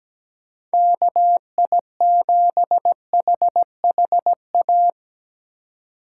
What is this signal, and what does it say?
Morse code, heard as a keyed tone, about 17 wpm, 700 Hz. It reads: KI7HHA
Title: USA Call Signs - 6 Characters - 4x Character Spacing - 17wpm